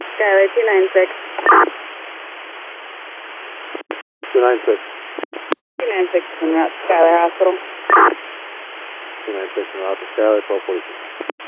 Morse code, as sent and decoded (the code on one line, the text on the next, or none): none